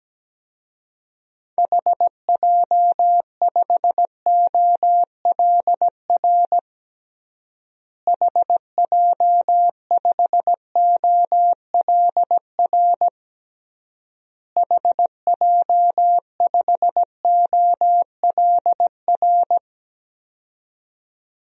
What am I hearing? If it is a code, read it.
Morse code, heard as a keyed tone, about 17 wpm, 700 Hz. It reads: HJ5OLR HJ5OLR HJ5OLR
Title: DXCC-100 Call Signs - 6 Characters - Repeated 3x - 17wpm